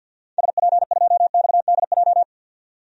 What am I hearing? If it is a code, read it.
Morse code, heard as a keyed tone, about 51 wpm, 700 Hz. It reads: SP2XDJ
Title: DXCC-100 Call Signs - 6 Characters - 50wpm